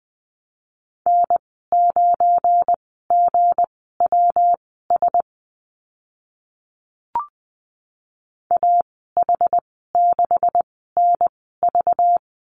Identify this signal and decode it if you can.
Morse code, heard as a keyed tone, about 20 wpm, spaced as for 14 wpm, 700 Hz. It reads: N9GWS AH6NV
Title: 193 Lesson 31g - Letter v - Call Signs 20wpm